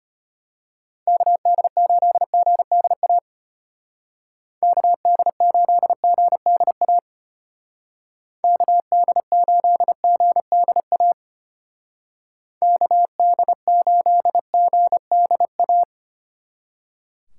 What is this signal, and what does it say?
Morse code, heard as a keyed tone, about 30 wpm, 700 Hz. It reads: KD8GDA KD8GDA KD8GDA KD8GDA
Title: Speed Racing - USA Call Signs 25wpm